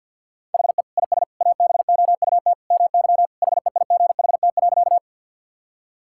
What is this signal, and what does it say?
Morse code, heard as a keyed tone, about 52 wpm, 700 Hz. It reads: HE IS ABOUT MY HEIGHT.
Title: Sentences from Top 1600 Words 50wpm